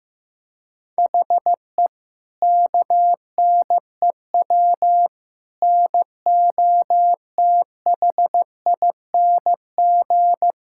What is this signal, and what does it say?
Morse code, heard as a keyed tone, about 16 wpm, 700 Hz. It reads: HE KNEW NOTHING
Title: Sentences from Top 1600 Words 15wpm